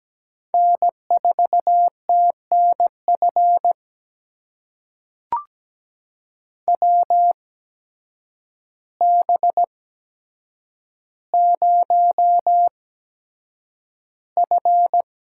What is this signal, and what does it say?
Morse code, heard as a keyed tone, about 17 wpm, spaced as for 8 wpm, 700 Hz. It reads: N4TNF WB0F
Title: USA Call Signs - 8x Character Spacing - 17wpm